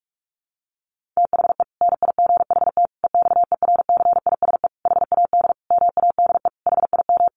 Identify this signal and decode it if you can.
Morse code, heard as a keyed tone, about 44 wpm, 700 Hz. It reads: THE NIGHT EXERCISE HAD MADE HIM
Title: Sentences from Top 1400 Words 45wpm